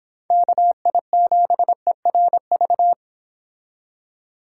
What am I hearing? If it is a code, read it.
Morse code, heard as a keyed tone, about 26 wpm, 700 Hz. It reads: KI7ERV